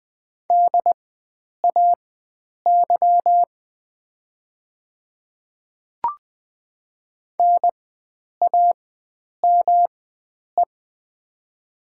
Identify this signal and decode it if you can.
Morse code, heard as a keyed tone, about 20 wpm, spaced as for 9 wpm, 700 Hz. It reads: DAY NAME
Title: 4x Character Spacing - Top 500 Words - 20wpm